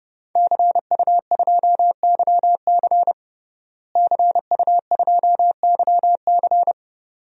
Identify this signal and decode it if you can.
Morse code, heard as a keyed tone, about 32 wpm, 700 Hz. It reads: CU2YC CU2YC
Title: DXCC-100 Call Signs - 5 Characters - Repeated 2x - 30wpm